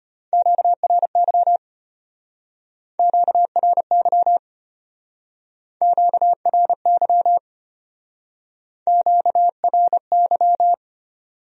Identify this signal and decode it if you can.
Morse code, heard as a keyed tone, about 31 wpm, 700 Hz. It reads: QRY QRY QRY QRY
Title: Speed-Racing Q-Codes - 25wpm